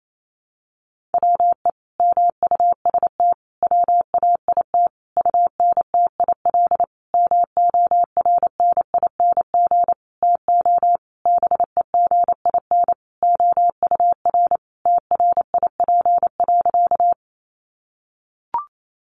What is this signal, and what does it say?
Morse code, heard as a keyed tone, about 28 wpm, 700 Hz. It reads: WE MUST WAIT UNTIL MORNING TO BEGIN OUR TRIP.
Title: Sentences from Top 1300 Words 28wpm